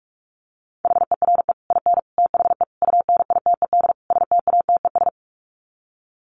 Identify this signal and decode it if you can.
Morse code, heard as a keyed tone, about 44 wpm, 700 Hz. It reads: HERE IN THE UNITED STATES
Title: Sets of 5 Words 45wpm